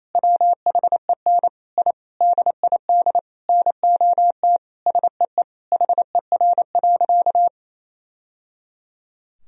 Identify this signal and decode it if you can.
Morse code, heard as a keyed tone, about 28 wpm, 700 Hz. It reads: WHEN I DID NOT SEE HER.
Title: Sentences from Top 1000 Words - Review Part 4 - 28wpm